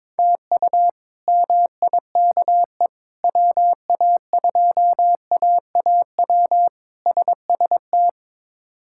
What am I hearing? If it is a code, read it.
Morse code, heard as a keyed tone, about 21 wpm, 700 Hz. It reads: TU MIKE WA2AAW SST
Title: K1USN SST QSO Practice - 22wpm